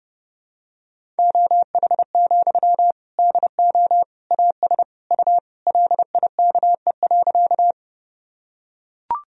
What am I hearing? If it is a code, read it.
Morse code, heard as a keyed tone, about 32 wpm, 700 Hz. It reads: OH, DO AS U LIKE.
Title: Sentences from Top 500 Words and Abbreviations 30wpm